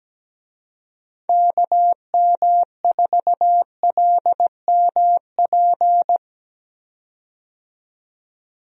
Morse code, heard as a keyed tone, about 17 wpm, 700 Hz. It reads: KM4LMP
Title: USA Call Signs - 6 Characters - 17wpm